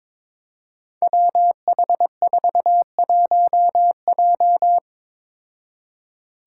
Morse code, heard as a keyed tone, about 22 wpm, 700 Hz. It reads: WH41J
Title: Five Challenging Letter-Number Combinations 22wpm